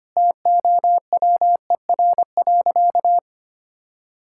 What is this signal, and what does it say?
Morse code, heard as a keyed tone, about 25 wpm, 700 Hz. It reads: TOWER.
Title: Pangram Challenge v2 - 25wpm